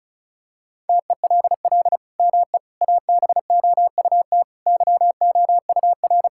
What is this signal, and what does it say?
Morse code, heard as a keyed tone, about 34 wpm, 700 Hz. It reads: TELL ME ABOUT YOUR
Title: Sentences from Top 1000 Words - Review Part 6 - 35wpm